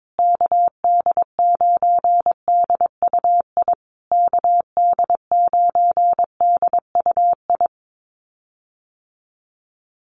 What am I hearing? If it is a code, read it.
Morse code, heard as a keyed tone, about 22 wpm, 700 Hz. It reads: KD9DUI KD9DUI
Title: USA Call Signs Repeated 3x 22wpm